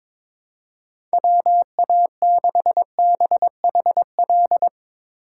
Morse code, heard as a keyed tone, about 22 wpm, 700 Hz. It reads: WA6BHL
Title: Speed Racing - USA 2x3 Call Signs 22wpm